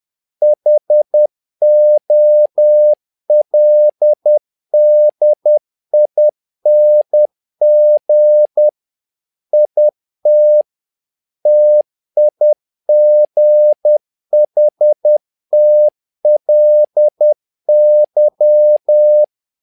Morse code, heard as a keyed tone, about 10 wpm, 600 Hz. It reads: HOLDING IT TIGHTLY
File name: learning center / tower trouble / tower trouble 10wpm